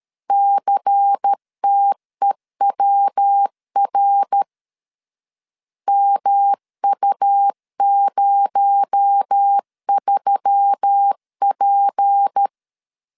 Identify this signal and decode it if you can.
Morse code, heard as a keyed tone, about 13 wpm, 800 Hz. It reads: CTEWR MU03P
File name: LEZIONI / 6 PKX0 / 6 PKX0 12wpm 2231